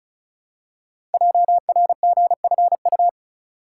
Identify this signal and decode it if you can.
Morse code, heard as a keyed tone, about 36 wpm, 700 Hz. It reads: JRGFU
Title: Code Groups - 5 Letters - 35wpm